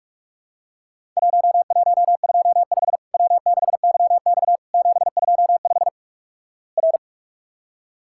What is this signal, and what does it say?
Morse code, heard as a keyed tone, about 46 wpm, 700 Hz. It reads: 1125 W6YX 725 R
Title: Worked All Europe DX Contest - QTC Exchange - 45wpm